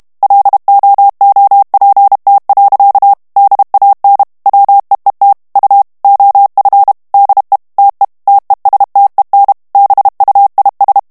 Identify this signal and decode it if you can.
Morse code, heard as a keyed tone, about 32 wpm, 800 Hz. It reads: LOOPT. DAN WEET U OF DE TE TESTEN BUIS
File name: HOLLANDAIS texte clair 2 vitesse 30 mots minute